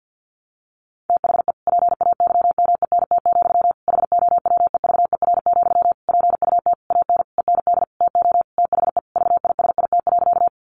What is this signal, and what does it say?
Morse code, heard as a keyed tone, about 50 wpm, 700 Hz. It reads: THE PAYMENT, HOWEVER, PUT AN END TO THE VISIT.
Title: Sentences from Top 1200 Words 50wpm